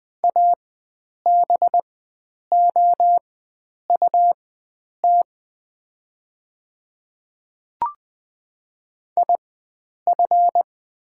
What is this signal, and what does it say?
Morse code, heard as a keyed tone, about 20 wpm, spaced as for 9 wpm, 700 Hz. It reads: ABOUT IF
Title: 4x Character Spacing - Top 100 Words - 20wpm